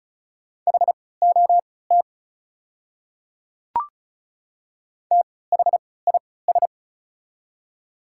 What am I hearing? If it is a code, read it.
Morse code, heard as a keyed tone, about 36 wpm, spaced as for 20 wpm, 700 Hz. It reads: HOT THIS
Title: 3x Character Spacing - Top 100 Words - 35wpm